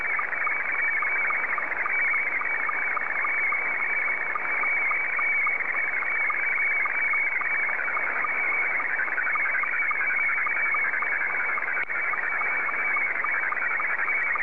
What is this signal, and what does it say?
SSTV (slow-scan television), Scottie 2.
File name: sstv / S2-JA3QJQ